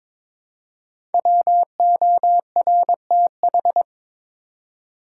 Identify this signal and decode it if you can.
Morse code, heard as a keyed tone, about 23 wpm, 700 Hz. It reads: WORTH
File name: 5 Letter Words - First 3 Letters Given - 22wpm